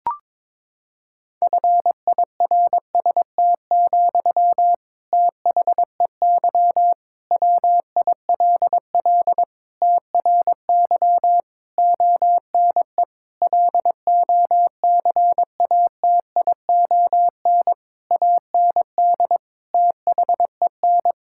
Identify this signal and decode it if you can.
Morse code, heard as a keyed tone, about 22 wpm, 700 Hz. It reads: FIRST, THEY WILL TRY ONE LOCATION AND THEN
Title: Sentences from Top 1000 Words - Review Part 6 - 22wpm